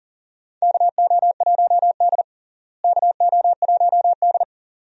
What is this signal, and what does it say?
Morse code, heard as a keyed tone, about 37 wpm, 700 Hz. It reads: KO1D KO1D
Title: USA Call Signs - 4 Characters - Repeated 2x - 40wpm